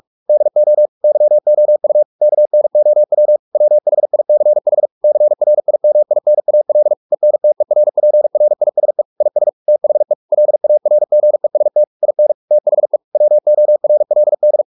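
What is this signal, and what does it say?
Morse code, heard as a keyed tone, about 44 wpm, 600 Hz. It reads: DO YOU KNOW WHICH CRIMINAL ENTERPRISE IS THE LARGEST IN THE WORLD
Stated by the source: Morse practice at 45 wpm, the novel Never by Ken Follett in English, from CW med Gnister.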